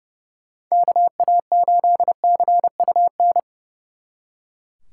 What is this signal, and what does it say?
Morse code, heard as a keyed tone, about 37 wpm, 700 Hz. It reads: KA8CUN
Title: Speed Racing - USA 2x3 Call Signs 30wpm